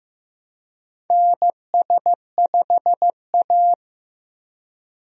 Morse code, heard as a keyed tone, about 15 wpm, 700 Hz. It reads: NS5A
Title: USA Call Signs - 4 Characters - Repeated 3x - 15wpm